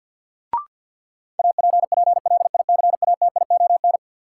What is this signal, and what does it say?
Morse code, heard as a keyed tone, about 48 wpm, 700 Hz. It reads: APPLICATION
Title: Sentences from Top 1200 Words - New Word Drill Alt - 50wpm